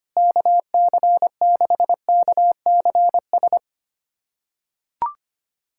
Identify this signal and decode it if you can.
Morse code, heard as a keyed tone, about 25 wpm, 700 Hz. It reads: KC6KCS
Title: USA Callsign DE Callsign - 6 Characters - 25wpm